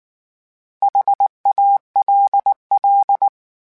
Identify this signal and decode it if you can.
Morse code, heard as a keyed tone, about 19 wpm, 800 Hz. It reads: HALL
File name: H Prefixes / H mp3s